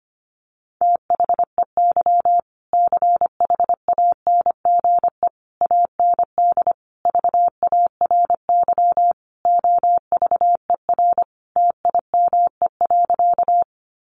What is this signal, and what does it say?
Morse code, heard as a keyed tone, about 25 wpm, 700 Hz. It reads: THEY CHANGE AND VARY OVER TIME.
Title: Sentences from Top 1200 Words 25wpm